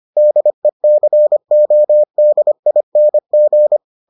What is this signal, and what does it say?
Morse code, heard as a keyed tone, about 25 wpm, 600 Hz. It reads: DECODING